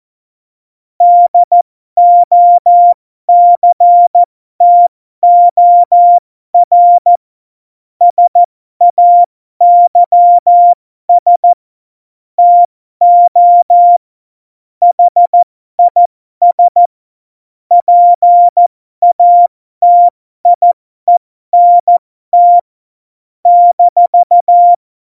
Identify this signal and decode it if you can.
Morse code, heard as a keyed tone, about 14 wpm, spaced as for 12 wpm, 700 Hz. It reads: DOCTOR SAYS TO HIS PATIENT -
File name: Morsekurs / Witze